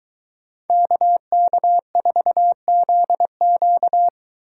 Morse code, heard as a keyed tone, about 23 wpm, 700 Hz. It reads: KK4ZQ